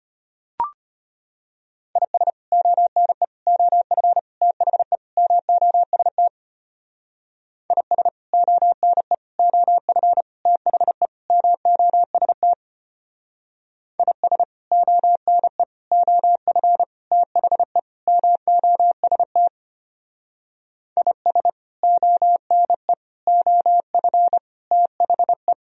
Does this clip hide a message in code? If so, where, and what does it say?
Yes: Morse code from 2.0 to 25.5 s, heard as a keyed tone, about 32 wpm, 700 Hz. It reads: IS ONE OF THE MOST IS ONE OF THE MOST IS ONE OF THE MOST IS ONE OF THE